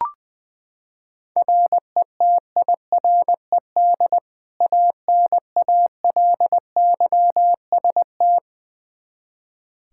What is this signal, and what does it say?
Morse code, heard as a keyed tone, about 21 wpm, 700 Hz. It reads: RETIRED ANALYST
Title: Practice QSO - Exchange 3 - More Info - 20wpm